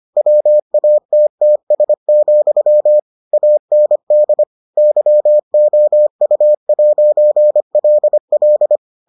Morse code, heard as a keyed tone, about 25 wpm, 600 Hz. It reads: WATTS, AND YOU'LL